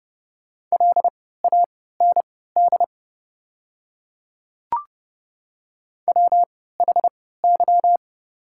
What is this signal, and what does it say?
Morse code, heard as a keyed tone, about 30 wpm, spaced as for 17 wpm, 700 Hz. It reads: LAND WHY
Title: 3x Character Spacing - Top 200 Words - 30wpm